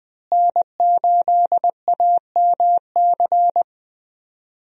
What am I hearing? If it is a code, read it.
Morse code, heard as a keyed tone, about 20 wpm, 700 Hz. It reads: N8AMC
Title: USA Call Signs - 5 Characters - Repeated 3x - 20wpm